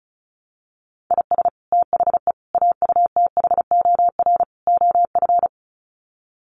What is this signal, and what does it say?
Morse code, heard as a keyed tone, about 34 wpm, 700 Hz. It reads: IS THE AUTHOR OF
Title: Sets of 4 Words 35wpm